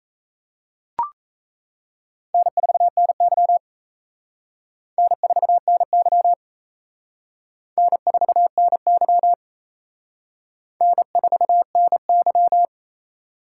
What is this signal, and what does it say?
Morse code, heard as a keyed tone, about 34 wpm, 700 Hz. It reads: N4NY N4NY N4NY N4NY
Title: Speed Racing - USA 1x2 Call Signs 28wpm